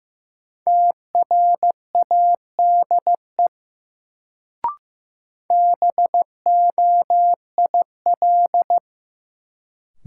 Morse code, heard as a keyed tone, about 15 wpm, 700 Hz. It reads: TRADE BOIL
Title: Sets of 5 Words Encore - New Word Drill - 15wpm